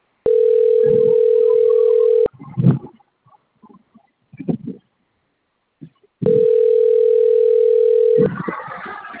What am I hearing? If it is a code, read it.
Ringback tone (North America), 440 + 480 Hz.